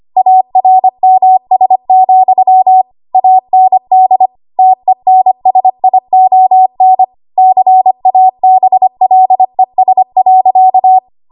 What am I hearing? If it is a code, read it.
Morse code, heard as a keyed tone, about 25 wpm, 750 Hz. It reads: ARMS, AND TENSION CABLES.
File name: Archive / 25 WPM / 25WPM